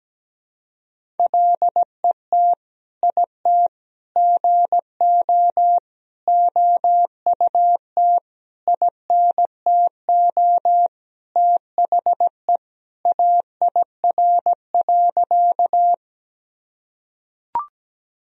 Morse code, heard as a keyed tone, about 17 wpm, 700 Hz. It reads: LET IT GO OUT INTO THE AIR.
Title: Sentences from Top 1000 Words - Review Part 1 - 17wpm